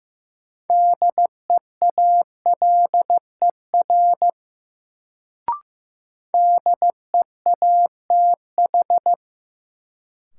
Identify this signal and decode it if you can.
Morse code, heard as a keyed tone, about 15 wpm, 700 Hz. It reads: DEALER DEATH
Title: Sentences from Top 1100 Words - New Word Drill - 15wpm